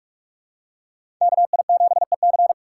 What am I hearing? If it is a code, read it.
Morse code, heard as a keyed tone, about 44 wpm, 700 Hz. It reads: KI7EC
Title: USA Call Signs - 5 Characters - 6x Character Spacing - 45wpm